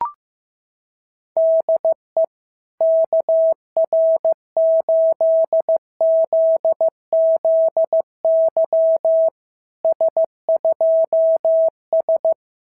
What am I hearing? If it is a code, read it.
Morse code, heard as a keyed tone, about 15 wpm, 650 Hz. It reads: DE KR8ZZY S2S
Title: SOTA QSO Practice 15wpm